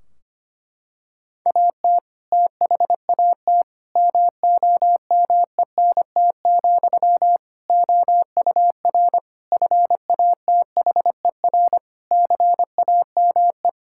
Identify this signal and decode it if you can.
Morse code, heard as a keyed tone, about 25 wpm, 700 Hz. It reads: AT THAT MOMENT, OUR FATHER CAME
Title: Sentences from Top 1600 Words 25wpm